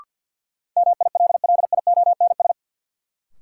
Morse code, heard as a keyed tone, about 54 wpm, 700 Hz. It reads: MILLIONS
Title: Sets of 5 Words - New Word Drill - 50wpm